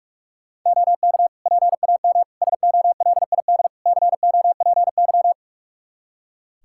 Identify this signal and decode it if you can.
Morse code, heard as a keyed tone, about 43 wpm, 700 Hz. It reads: OK PAM SOLID COPY
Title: Practice QSO - Exchange 3 - More Info - 45wpm